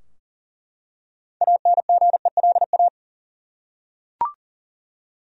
Morse code, heard as a keyed tone, about 40 wpm, 700 Hz. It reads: ANGELA
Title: USA Common Names - 40wpm